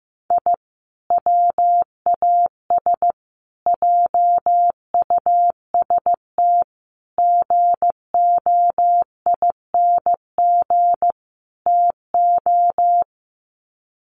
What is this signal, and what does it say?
Morse code, heard as a keyed tone, about 15 wpm, 700 Hz. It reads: I WAS JUST GOING TO
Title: Sets of 5 Words Encore 15wpm